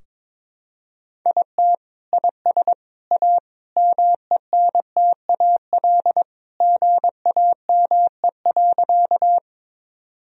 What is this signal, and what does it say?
Morse code, heard as a keyed tone, about 22 wpm, 700 Hz. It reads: IT IS A MENTAL GAME.
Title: Sentences from Top 1000 Words - Review Part 4 - 22wpm